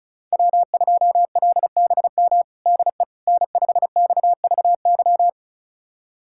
Morse code, heard as a keyed tone, about 34 wpm, 700 Hz. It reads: W2LBM DE N5XVY
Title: USA Callsign DE Callsign 35wpm